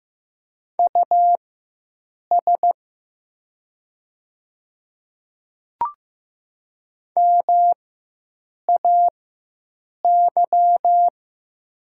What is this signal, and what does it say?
Morse code, heard as a keyed tone, about 15 wpm, spaced as for 7 wpm, 700 Hz. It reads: US MAY
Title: 4x Character Spacing - Top 200 Words - 15wpm